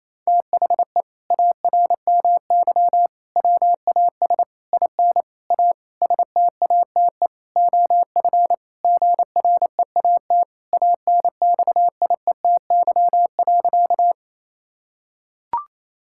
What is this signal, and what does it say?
Morse code, heard as a keyed tone, about 27 wpm, 700 Hz. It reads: THE ARMY WAS IN A STATE OF GREAT ANXIETY.
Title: Sentences from Top 1600 Words 28wpm